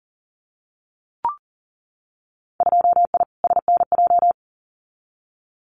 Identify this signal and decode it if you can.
Morse code, heard as a keyed tone, about 40 wpm, 700 Hz. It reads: 2I SNJ
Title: Winter Field Day Exchanges Encore 40wpm